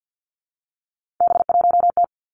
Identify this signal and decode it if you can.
Morse code, heard as a keyed tone, about 50 wpm, 700 Hz. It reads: B1T
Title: Three Challenging Letter-Number Combinations 50wpm